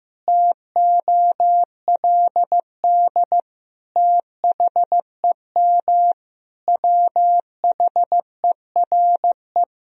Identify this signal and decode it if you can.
Morse code, heard as a keyed tone, about 15 wpm, 700 Hz. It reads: TOLD THEM WHERE
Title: Sentences from Top 1000 Words - Review Part 4 - 15wpm